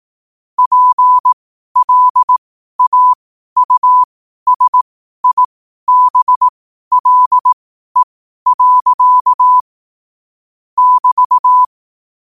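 Morse code, heard as a keyed tone, about 18 wpm, spaced as for 12 wpm, 1000 Hz. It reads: PLAUSIBLE. <BT>
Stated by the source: Morse code sent at 12 words per minute.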